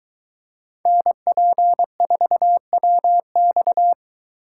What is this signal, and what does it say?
Morse code, heard as a keyed tone, about 22 wpm, 700 Hz. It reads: NP4WX